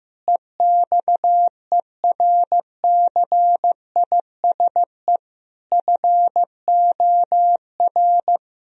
Morse code, heard as a keyed tone, about 15 wpm, 700 Hz. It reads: EXERCISE FOR
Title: Sentences from Top 1000 Words - Review Part 4 - 15wpm